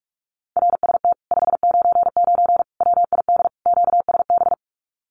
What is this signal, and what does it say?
Morse code, heard as a keyed tone, about 44 wpm, 700 Hz. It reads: RST 599 WID QSB
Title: Common QSO Phrases Encore 45wpm